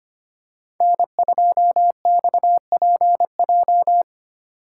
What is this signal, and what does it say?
Morse code, heard as a keyed tone, about 31 wpm, 700 Hz. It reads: N2XPJ
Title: Speed Racing - USA 1x3 Call Signs 25wpm